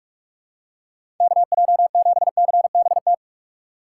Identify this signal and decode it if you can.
Morse code, heard as a keyed tone, about 44 wpm, 700 Hz. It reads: KJ7CBT
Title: USA Call Signs - 6 Characters - 6x Character Spacing - 45wpm